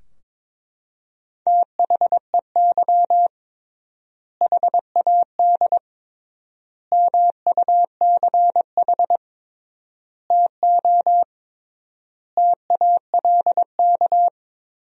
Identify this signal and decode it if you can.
Morse code, heard as a keyed tone, about 22 wpm, 700 Hz. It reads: THEY HAD MUCH TO TALK
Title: 3x Word-Spacing - Sentences from Top 400 Words - 22wpm